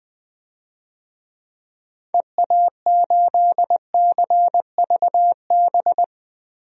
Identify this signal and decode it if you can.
Morse code, heard as a keyed tone, about 20 wpm, 700 Hz. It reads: EA8CVB
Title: DXCC-100 Call Signs - 6 Characters - 20wpm